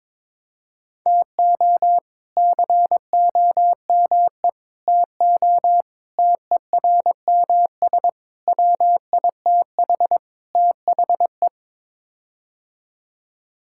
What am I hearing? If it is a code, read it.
Morse code, heard as a keyed tone, about 21 wpm, 700 Hz. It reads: TO COME TO TERMS WITH THE
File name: practice / 22wpm / Sets of 6 Words - 22wpm